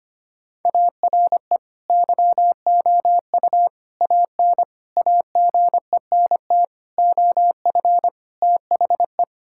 Morse code, heard as a keyed tone, about 25 wpm, 700 Hz. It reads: ARE YOU AN AGENT OF THE
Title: Sentences from Top 1600 Words 25wpm